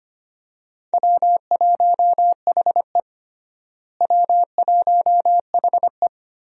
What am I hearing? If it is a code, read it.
Morse code, heard as a keyed tone, about 25 wpm, 700 Hz. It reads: W1HE W1HE
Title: USA Call Signs - 4 Characters - Repeated 3x - 25wpm